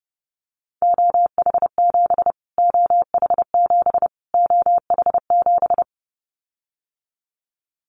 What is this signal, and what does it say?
Morse code, heard as a keyed tone, about 30 wpm, 700 Hz. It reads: OH7 OH7 OH7
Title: Callsign Prefixes for Top 25 DXCC - Repeated 3x - 30wpm